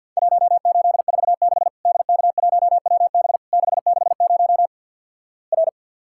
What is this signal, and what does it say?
Morse code, heard as a keyed tone, about 52 wpm, 700 Hz. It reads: 1846 DK1WB 660 R
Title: Worked All Europe DX Contest - QTC Exchange - 50wpm